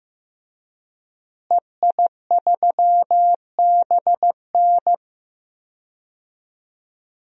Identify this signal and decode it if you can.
Morse code, heard as a keyed tone, about 15 wpm, 700 Hz. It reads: EI3BN
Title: DXCC-100 Call Signs - 5 Characters - Repeated 3x - 15wpm